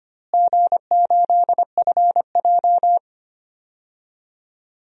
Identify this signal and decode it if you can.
Morse code, heard as a keyed tone, about 25 wpm, 700 Hz. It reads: G8FJ
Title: DXCC-100 Call Signs - 4 Characters - Repeated 3x - 25wpm